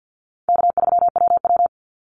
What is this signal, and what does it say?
Morse code, heard as a keyed tone, about 50 wpm, 700 Hz. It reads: K3WW
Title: Easy Call Signs Repeated 2x - 50wpm